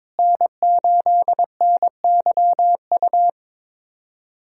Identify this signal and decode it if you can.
Morse code, heard as a keyed tone, about 22 wpm, 700 Hz. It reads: N8NYU
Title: USA Callsign DE Callsign 22wpm